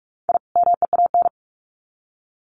Morse code, heard as a keyed tone, about 44 wpm, 700 Hz. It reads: I MEAN
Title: Sets of 2 Words 45wpm